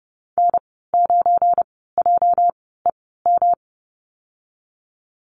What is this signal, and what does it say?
Morse code, heard as a keyed tone, about 30 wpm, spaced as for 17 wpm, 700 Hz. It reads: N9JEM
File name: Lesson 37g - Letter x - Call Signs 30wpm